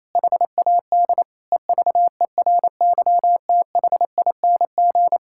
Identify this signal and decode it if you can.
Morse code, heard as a keyed tone, about 27 wpm, 700 Hz. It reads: HAD EVERYTHING
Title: Sentences from Top 1000 Words - Review Part 5 - 28wpm